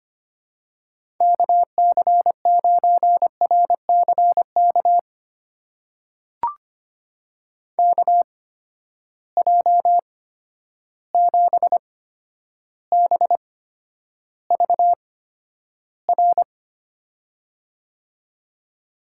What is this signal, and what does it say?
Morse code, heard as a keyed tone, about 25 wpm, spaced as for 11 wpm, 700 Hz. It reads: KC9RCK KJ7BVR
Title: USA Call Signs - 6 Characters - 8x Character Spacing - 25wpm